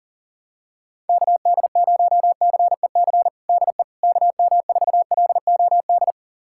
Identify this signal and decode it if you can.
Morse code, heard as a keyed tone, about 39 wpm, 700 Hz. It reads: KD0CEC DE KM4LOD
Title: USA Callsign DE Callsign - 6 Characters - 40wpm